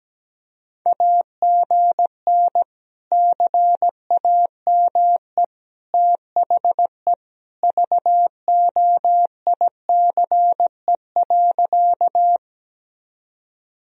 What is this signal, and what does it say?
Morse code, heard as a keyed tone, about 17 wpm, 700 Hz. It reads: AGN CAME THE VOICE.
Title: Sentences from Top 500 Words and Abbreviations 17wpm